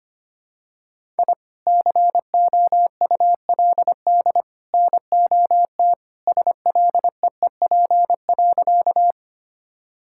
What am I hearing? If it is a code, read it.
Morse code, heard as a keyed tone, about 25 wpm, 700 Hz. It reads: I COULD NOT SLEEP.
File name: practice / 25wpm / Sentences with 3-Word Phrases - 25wpm